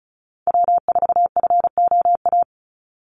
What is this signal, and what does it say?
Morse code, heard as a keyed tone, about 34 wpm, 700 Hz. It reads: W4FOA
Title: Call Signs - 35wpm